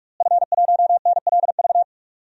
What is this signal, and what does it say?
Morse code, heard as a keyed tone, about 42 wpm, 700 Hz. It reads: F1NLV
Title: DXCC-100 Call Signs - 5 Characters - Repeated 3x - 45wpm